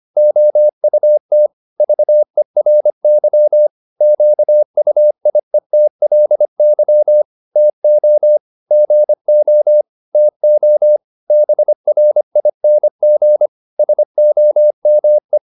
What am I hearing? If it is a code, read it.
Morse code, heard as a keyed tone, about 25 wpm, 600 Hz. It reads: OUT VERY QUIETLY TO GO TO BRING SOME